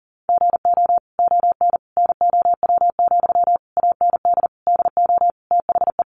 Morse code, heard as a keyed tone, about 40 wpm, 700 Hz. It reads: GO ON NOW, AND DO THE
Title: Sentences from Top 1500 Words 40wpm